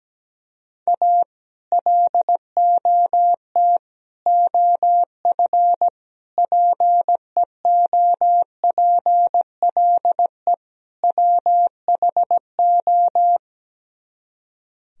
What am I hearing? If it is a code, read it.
Morse code, heard as a keyed tone, about 17 wpm, 700 Hz. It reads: A LOT OF PEOPLE WHO
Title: Speed-Racing - Sets of 5 Words 17wpm